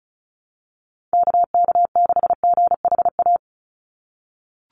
Morse code, heard as a keyed tone, about 34 wpm, 700 Hz. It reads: KK6GHA
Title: Call Signs - 35wpm